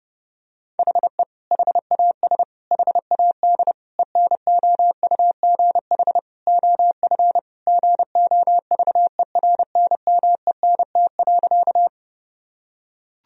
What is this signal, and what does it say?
Morse code, heard as a keyed tone, about 32 wpm, 700 Hz. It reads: HE HAS HAD ENOUGH OF GOVERNMENT.
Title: Sentences from Top 1000 Words - Review Part 3 - 30wpm